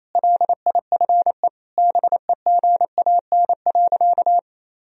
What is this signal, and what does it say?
Morse code, heard as a keyed tone, about 28 wpm, 700 Hz. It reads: LIFE BEGAN.
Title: Sentences from Top 1000 Words - Review Part 1 - 28wpm